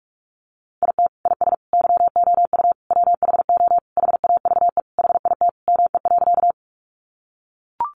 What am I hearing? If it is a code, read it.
Morse code, heard as a keyed tone, about 44 wpm, 700 Hz. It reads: IT IS YOU WHO HAVE HIT ME.
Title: Sentences from Top 1200 Words 45wpm